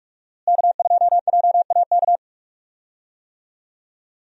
Morse code, heard as a keyed tone, about 42 wpm, 700 Hz. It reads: K2JAK
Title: USA Call Signs - 5 Characters - Repeated 2x - 45wpm